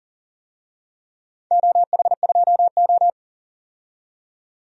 Morse code, heard as a keyed tone, about 41 wpm, 700 Hz. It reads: OH2O
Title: DXCC-100 Call Signs - 4 Characters - 40wpm